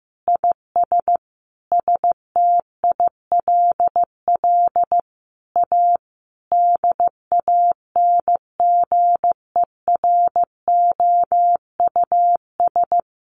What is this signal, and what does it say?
Morse code, heard as a keyed tone, about 15 wpm, 700 Hz. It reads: IS STILL A DANGEROUS
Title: Sentences from Top 1200 Words 15wpm